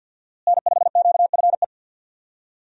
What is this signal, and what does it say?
Morse code, heard as a keyed tone, about 48 wpm, 700 Hz. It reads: N5QFE
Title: USA Call Signs - 5 Characters - Repeated 3x - 50wpm